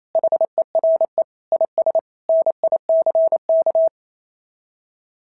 Morse code, heard as a keyed tone, about 27 wpm, 690 Hz. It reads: HERE IS NICK
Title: SOTA QSO Practice 28wpm